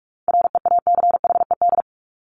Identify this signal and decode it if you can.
Morse code, heard as a keyed tone, about 44 wpm, 700 Hz. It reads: REACHED
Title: ED Suffix - Top 100 Words with Base Words 45wpm